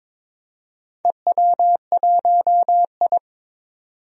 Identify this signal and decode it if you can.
Morse code, heard as a keyed tone, about 23 wpm, 700 Hz. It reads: EW1I
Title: DXCC-100 Call Signs - 4 Characters - 22wpm